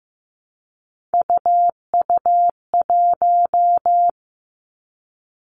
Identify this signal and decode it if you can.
Morse code, heard as a keyed tone, about 15 wpm, 700 Hz. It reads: UU1
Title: Three Challenging Letter-Number Combinations 15wpm